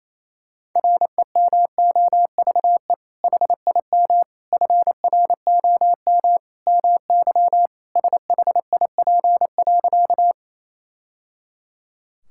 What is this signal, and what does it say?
Morse code, heard as a keyed tone, about 28 wpm, 700 Hz. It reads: REMOVE HIM FROM MY SHIP.
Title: Sentences from Top 1000 Words - Review Part 5 - 28wpm